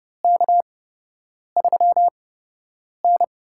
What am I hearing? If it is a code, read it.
Morse code, heard as a keyed tone, about 32 wpm, spaced as for 8 wpm, 700 Hz. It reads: K3N
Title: USA Call Signs - 4 Characters - 8x Character Spacing - 30wpm